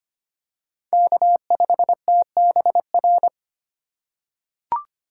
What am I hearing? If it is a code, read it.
Morse code, heard as a keyed tone, about 25 wpm, 700 Hz. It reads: K5TBR
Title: USA Call Signs - 5 Characters - 4x Character Spacing - 25wpm